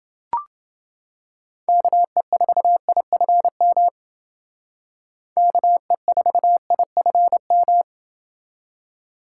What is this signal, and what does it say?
Morse code, heard as a keyed tone, about 29 wpm, 700 Hz. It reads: KE4IFM KE4IFM